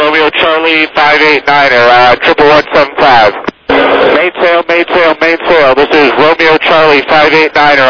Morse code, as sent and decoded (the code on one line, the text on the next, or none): none